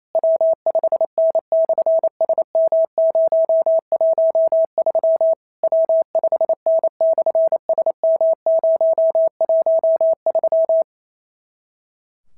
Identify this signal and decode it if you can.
Morse code, heard as a keyed tone, about 28 wpm, 650 Hz. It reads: W5N/SM013 W5N/SM013
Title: SOTA QSO Practice 28wpm